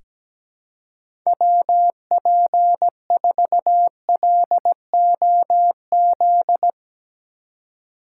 Morse code, heard as a keyed tone, about 17 wpm, 700 Hz. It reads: WP4LOZ
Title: USA Call Signs - 6 Characters - 3x Character Spacing - 17wpm